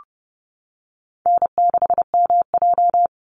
Morse code, heard as a keyed tone, about 30 wpm, 700 Hz. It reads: N6MJ
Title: Easy Call Signs - 30wpm